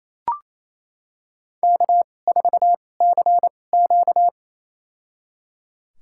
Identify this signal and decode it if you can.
Morse code, heard as a keyed tone, about 28 wpm, spaced as for 20 wpm, 700 Hz. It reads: K4CQ
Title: USA Call Signs - 4 Characters - 2x Character Spacing - 28wpm